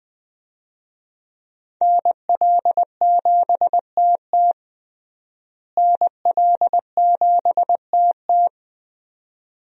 Morse code, heard as a keyed tone, about 20 wpm, 700 Hz. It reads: NL7TT NL7TT
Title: USA Call Signs - 5 Characters - Repeated 2x - 20wpm